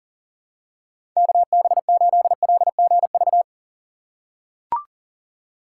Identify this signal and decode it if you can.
Morse code, heard as a keyed tone, about 40 wpm, 700 Hz. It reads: KB8LGV
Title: USA Call Signs - 6 Characters - 2x Character Spacing - 40wpm